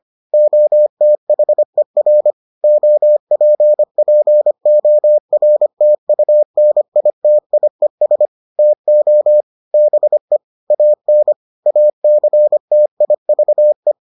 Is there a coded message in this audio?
Yes: Morse code from 0.3 to 13.9 s, heard as a keyed tone, about 25 wpm, 600 Hz. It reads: OTHER OPPORTUNITIES TO BE AN ACTIVE